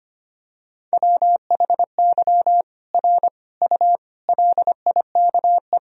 Morse code, heard as a keyed tone, about 25 wpm, 700 Hz. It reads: WHY R U LIKE
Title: Sentences from Top 200 Words and Abbreviations 25wpm